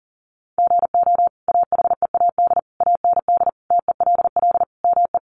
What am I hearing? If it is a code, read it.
Morse code, heard as a keyed tone, about 40 wpm, 700 Hz. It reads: GO AHEAD AND TELL ME
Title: Sentences from Top 1100 Words 40wpm